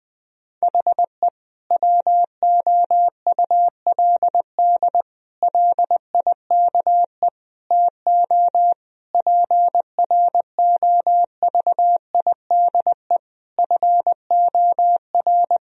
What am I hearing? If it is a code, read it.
Morse code, heard as a keyed tone, about 20 wpm, 700 Hz. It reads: HE WOULD LIKE TO PROVIDE FOR
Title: Sentences from Top 1000 Words - Review Part 3 - 20wpm